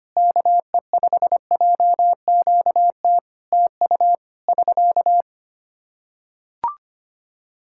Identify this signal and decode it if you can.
Morse code, heard as a keyed tone, about 25 wpm, 700 Hz. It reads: KE5JQT TU <SK>
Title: Practice QSO - Exchange 4 - Ending - 25wpm